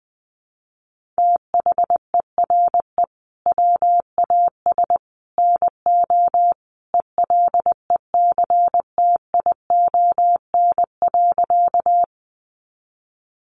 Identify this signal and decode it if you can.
Morse code, heard as a keyed tone, about 20 wpm, 700 Hz. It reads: THERE WAS NO ELECTION.
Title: Sentences from Top 1500 Words 20wpm